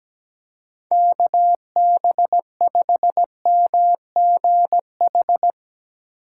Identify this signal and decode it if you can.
Morse code, heard as a keyed tone, about 17 wpm, 700 Hz. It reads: KB5MGH
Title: USA Call Signs - 6 Characters - 4x Character Spacing - 17wpm